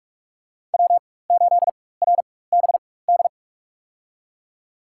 Morse code, heard as a keyed tone, about 46 wpm, spaced as for 21 wpm, 700 Hz. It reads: W8RBD
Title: USA Call Signs - 5 Characters - 4x Character Spacing - 45wpm